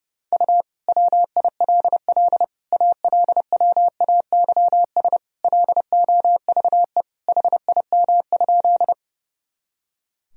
Morse code, heard as a keyed tone, about 32 wpm, 700 Hz. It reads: U WILL ALWAYS LOVE HIM?
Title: Sentences from Top 400 Words and Abbreviations 30wpm